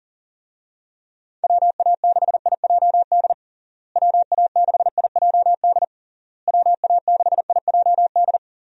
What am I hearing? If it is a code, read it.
Morse code, heard as a keyed tone, about 37 wpm, 700 Hz. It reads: WA6IJD WA6IJD WA6IJD
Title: USA Call Signs - 6 Characters - Repeated 3x - 40wpm